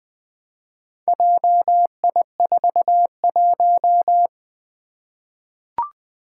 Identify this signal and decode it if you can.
Morse code, heard as a keyed tone, about 20 wpm, 700 Hz. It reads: JI41
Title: Four Challenging Letter-Number Combinations 20wpm